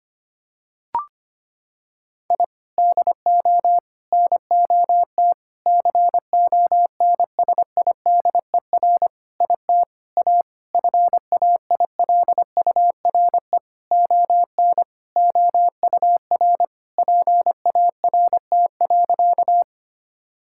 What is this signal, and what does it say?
Morse code, heard as a keyed tone, about 25 wpm, 700 Hz. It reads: I DO NOT CONSIDER IT A FAILURE ON OUR PART.
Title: Sentences from Top 1000 Words - Review Part 6 - 25wpm